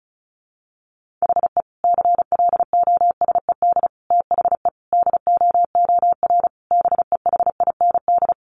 Morse code, heard as a keyed tone, about 34 wpm, 700 Hz. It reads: HE CLOSED THE DOOR BEHIND
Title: Sentences from Top 1400 Words 35wpm